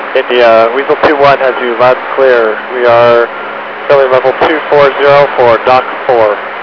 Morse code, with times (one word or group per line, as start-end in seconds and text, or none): none